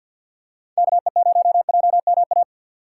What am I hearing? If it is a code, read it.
Morse code, heard as a keyed tone, about 51 wpm, 700 Hz. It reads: KE0JMA
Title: USA Call Signs - 6 Characters - 3x Character Spacing - 50wpm